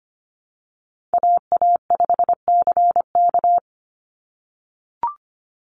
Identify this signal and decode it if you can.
Morse code, heard as a keyed tone, about 25 wpm, 700 Hz. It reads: AA5CK
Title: NA SOTA Chaser Callsigns 25wpm